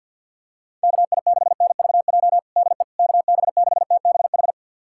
Morse code, heard as a keyed tone, about 48 wpm, 700 Hz. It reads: KI6NVJ DE KB6TBH
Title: USA Callsign DE Callsign - 6 Characters - 50wpm